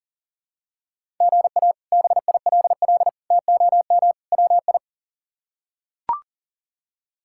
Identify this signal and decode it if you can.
Morse code, heard as a keyed tone, about 40 wpm, 700 Hz. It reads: GA BILL TOM WI
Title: K1USN SST QSO Practice - 40wpm